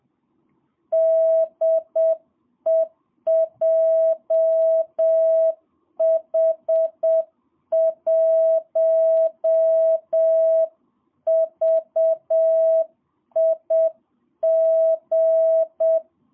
Morse code, heard as a keyed tone, about 7 wpm, 660 Hz. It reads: DE JH1VIG
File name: jh8cbh / 2022cw